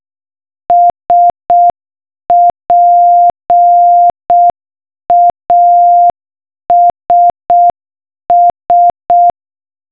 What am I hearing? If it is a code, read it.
Morse code, heard as a keyed tone, about 6 wpm, 700 Hz. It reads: SPASS
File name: morsecode-knacken